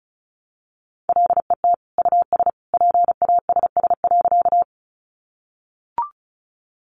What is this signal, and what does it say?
Morse code, heard as a keyed tone, about 34 wpm, 700 Hz. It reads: LET US PASS.
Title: Sentences from Top 400 Words 35wpm